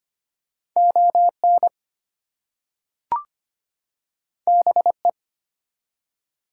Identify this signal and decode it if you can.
Morse code, heard as a keyed tone, about 25 wpm, 700 Hz. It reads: ON BE
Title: Two Letter Words - 25wpm